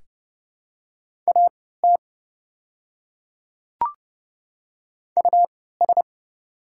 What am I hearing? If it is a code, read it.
Morse code, heard as a keyed tone, about 30 wpm, spaced as for 17 wpm, 700 Hz. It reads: AT US
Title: 3x Character Spacing - Top 300 Words - 30wpm